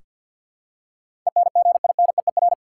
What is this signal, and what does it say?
Morse code, heard as a keyed tone, about 47 wpm, 700 Hz. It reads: ENGINEER
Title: Practice QSO - Exchange 3 - More Info - 50wpm